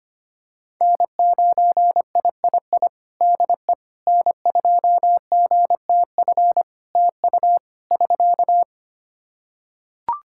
Morse code, heard as a keyed tone, about 25 wpm, 700 Hz. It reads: N9III DE N2GTF TU <SK>
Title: Practice QSO - Exchange 4 - Ending - 25wpm